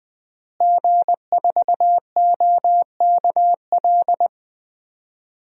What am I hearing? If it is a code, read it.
Morse code, heard as a keyed tone, about 21 wpm, 700 Hz. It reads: G4OKL
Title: DXCC-100 Call Signs - 5 Characters - Repeated 3x - 20wpm